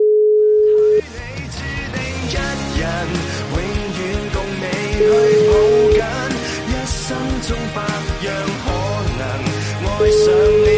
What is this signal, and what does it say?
Ringback tone (Europe, 425 Hz), with noise.